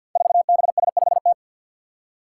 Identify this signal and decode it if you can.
Morse code, heard as a keyed tone, about 47 wpm, 700 Hz. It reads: 4BS5T
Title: Five Challenging Letter-Number Combinations 50wpm